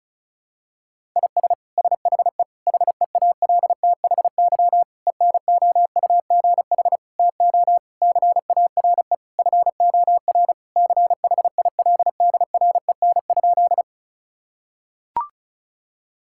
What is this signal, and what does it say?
Morse code, heard as a keyed tone, about 34 wpm, 700 Hz. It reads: IS SHE HEALTHY ENOUGH TO CARE FOR CHILDREN?
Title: Sentences from Top 1000 Words - Review Part 5 - 35wpm